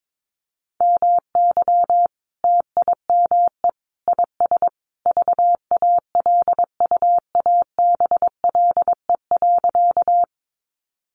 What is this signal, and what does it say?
Morse code, heard as a keyed tone, about 22 wpm, 700 Hz. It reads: MY TIME IS VALUABLE.
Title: Sentences from Top 1100 Words 22wpm